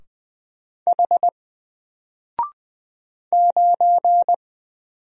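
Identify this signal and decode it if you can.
Morse code, heard as a keyed tone, about 20 wpm, 700 Hz. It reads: H 9
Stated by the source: Single letter and number Morse drill at 20 wpm, from N4LCD.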